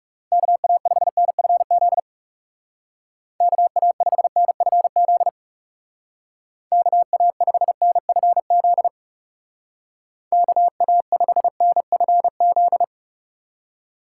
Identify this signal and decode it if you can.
Morse code, heard as a keyed tone, about 36 wpm, 700 Hz. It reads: KA5NFZ KA5NFZ KA5NFZ KA5NFZ
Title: Speed Racing - USA 2x3 Call Signs 30wpm